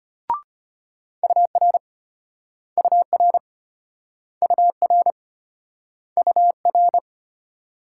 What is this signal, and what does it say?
Morse code, heard as a keyed tone, about 34 wpm, 700 Hz. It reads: UR UR UR UR